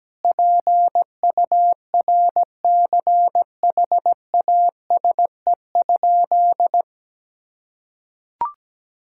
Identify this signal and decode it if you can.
Morse code, heard as a keyed tone, about 17 wpm, 700 Hz. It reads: PURCHASE?
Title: Sentences from Top 1000 Words - Review Part 6 - 17wpm